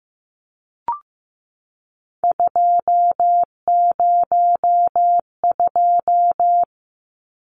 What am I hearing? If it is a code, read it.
Morse code, heard as a keyed tone, about 15 wpm, 700 Hz. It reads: 202
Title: Numbers - 3-Digit 15wpm